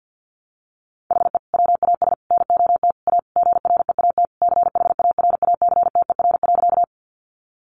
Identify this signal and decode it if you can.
Morse code, heard as a keyed tone, about 50 wpm, 700 Hz. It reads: HE WAS NOT A GREAT CHARACTER.
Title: Sentences from Top 1200 Words 50wpm